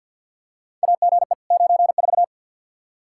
Morse code, heard as a keyed tone, about 51 wpm, 700 Hz. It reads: AGE 94
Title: K1USN SST QSO Practice 50wpm